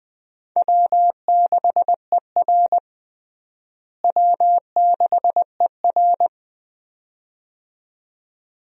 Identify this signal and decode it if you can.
Morse code, heard as a keyed tone, about 20 wpm, 700 Hz. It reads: W6ER W6ER
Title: USA Call Signs - 4 Characters - Repeated 2x - 20wpm